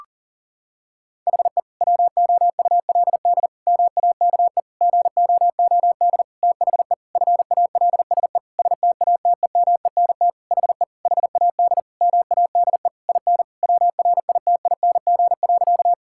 Morse code, heard as a keyed tone, about 37 wpm, 700 Hz. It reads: HE WOULD MAKE GOOD THE FALSE STATEMENT HE HAD MADE IN WRITING.
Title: Sentences from Top 1600 Words 40wpm